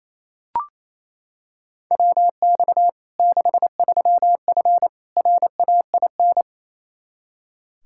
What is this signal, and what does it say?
Morse code, heard as a keyed tone, about 28 wpm, 700 Hz. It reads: WX 63F RAIN
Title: Practice QSO - Exchange 2 - Info - 28wpm